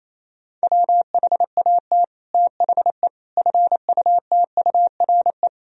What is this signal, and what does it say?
Morse code, heard as a keyed tone, about 28 wpm, 700 Hz. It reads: WHAT THE FUTURE
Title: Sentences from Top 1000 Words - Review Part 2 - 28wpm